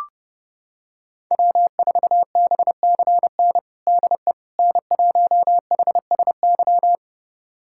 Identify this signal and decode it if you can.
Morse code, heard as a keyed tone, about 30 wpm, 700 Hz. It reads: W4BCN DE N1HSY
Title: USA Callsign DE Callsign - 5 Characters - 30wpm